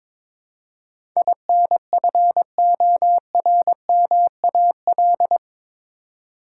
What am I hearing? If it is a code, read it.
Morse code, heard as a keyed tone, about 21 wpm, 700 Hz. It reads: INFORMAL
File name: practice / 22wpm / Eight Letter Words - 22wpm